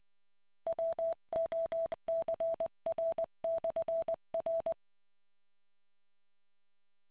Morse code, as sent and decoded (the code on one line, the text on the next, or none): .-- --- -.-. .-. -..-. .-.
WOCR/R